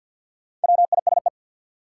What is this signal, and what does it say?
Morse code, heard as a keyed tone, about 47 wpm, 700 Hz. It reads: WISE
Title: Sentences from Top 1600 Words - New Word Drill - 50wpm